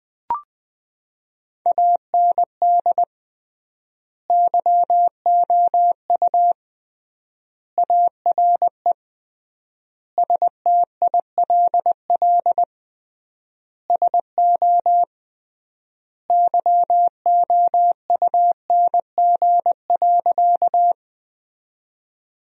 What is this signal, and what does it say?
Morse code, heard as a keyed tone, about 21 wpm, 700 Hz. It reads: AND YOU ARE STILL SO YOUNG.
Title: 3x Word-Spacing - Sentences from Top 400 Words - 20wpm